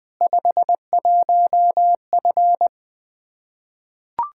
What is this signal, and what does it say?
Morse code, heard as a keyed tone, about 20 wpm, 700 Hz. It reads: 51F
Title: Practice QSO - Exchange 2 - Info - 20wpm